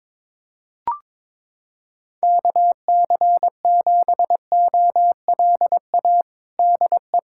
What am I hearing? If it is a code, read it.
Morse code, heard as a keyed tone, about 22 wpm, 700 Hz. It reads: KC7OLA DE
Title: USA Callsign DE Callsign 22wpm